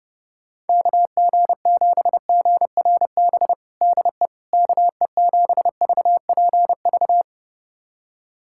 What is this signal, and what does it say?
Morse code, heard as a keyed tone, about 30 wpm, 700 Hz. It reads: KG7GRB DE KE7VPV
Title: USA Callsign DE Callsign - 6 Characters - 30wpm